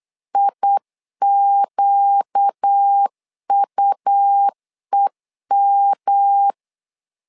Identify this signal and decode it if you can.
Morse code, heard as a keyed tone, about 8 wpm, 800 Hz. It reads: IQUEM
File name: LEZIONI / 4 Q 8wpm 2732